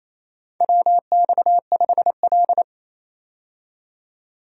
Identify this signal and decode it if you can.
Morse code, heard as a keyed tone, about 28 wpm, 700 Hz. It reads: WX5L
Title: USA Call Signs - 4 Characters - 28wpm